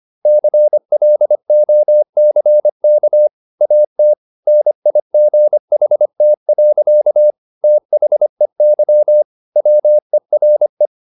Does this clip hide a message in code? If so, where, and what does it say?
Yes: Morse code from 0.3 to 10.9 s, heard as a keyed tone, about 25 wpm, 600 Hz. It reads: CLOCK AT NIGHT. THEY WERE